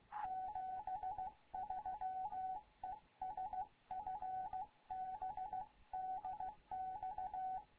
Morse code, heard as a keyed tone, about 16 wpm, 750 Hz. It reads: 73 ES FB DX
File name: Morze-HA2MN / hangp / 2 11 GYAKORLAT